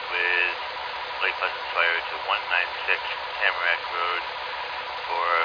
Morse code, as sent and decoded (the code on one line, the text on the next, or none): none